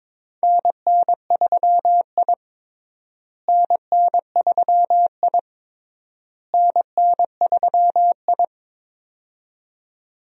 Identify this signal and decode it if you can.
Morse code, heard as a keyed tone, about 22 wpm, 700 Hz. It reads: NN3I NN3I NN3I
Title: USA Call Signs - 4 Characters - Repeated 3x - 22wpm